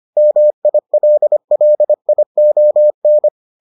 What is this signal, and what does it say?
Morse code, heard as a keyed tone, about 25 wpm, 600 Hz. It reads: MILLION